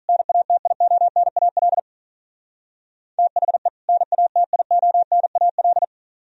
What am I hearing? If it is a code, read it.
Morse code, heard as a keyed tone, about 45 wpm, 700 Hz. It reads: NATIONAL THE NATIONAL